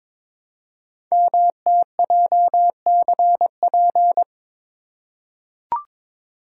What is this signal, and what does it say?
Morse code, heard as a keyed tone, about 22 wpm, 700 Hz. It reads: MTJCP
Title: Code Groups - 5 Letters - 22wpm